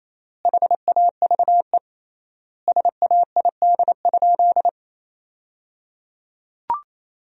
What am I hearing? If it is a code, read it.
Morse code, heard as a keyed tone, about 28 wpm, 700 Hz. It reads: HAVE SAID?
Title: 3x Word-Spacing - Sentences from Top 100 Words - 28wpm